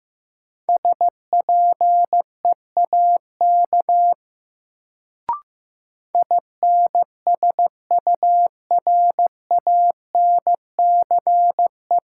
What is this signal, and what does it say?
Morse code, heard as a keyed tone, about 15 wpm, 700 Hz. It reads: SPEAK INSURANCE
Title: Sentences from Top 1100 Words - New Word Drill Alt - 15wpm